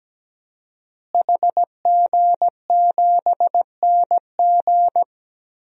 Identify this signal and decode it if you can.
Morse code, heard as a keyed tone, about 17 wpm, 700 Hz. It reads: HG7NG
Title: DXCC-100 Call Signs - 5 Characters - Repeated 2x - 17wpm